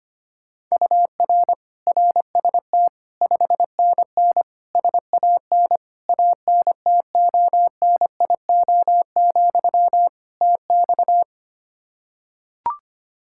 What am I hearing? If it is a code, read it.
Morse code, heard as a keyed tone, about 25 wpm, 700 Hz. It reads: UR RST 5NN SAN ANTONIO, TX
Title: Practice QSO - Exchange 1 - Beginning - 25wpm